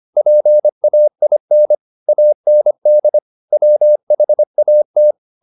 Morse code, heard as a keyed tone, about 25 wpm, 600 Hz. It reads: PAIN AND WHAT